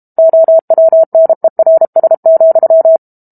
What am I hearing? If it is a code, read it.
Morse code, heard as a keyed tone, about 32 wpm, 650 Hz. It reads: OWNERS,
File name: Book Two / 06-Chapter 6 - 32 WPM